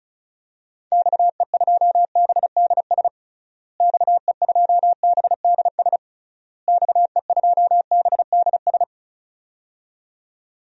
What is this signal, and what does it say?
Morse code, heard as a keyed tone, about 34 wpm, 700 Hz. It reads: XE2BDS XE2BDS XE2BDS
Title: DXCC-100 Call Signs - 6 Characters - Repeated 3x - 35wpm